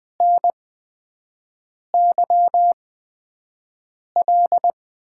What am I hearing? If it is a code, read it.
Morse code, heard as a keyed tone, about 21 wpm, spaced as for 5 wpm, 700 Hz. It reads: NYL
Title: USA Call Signs - 6 Characters - 8x Character Spacing - 20wpm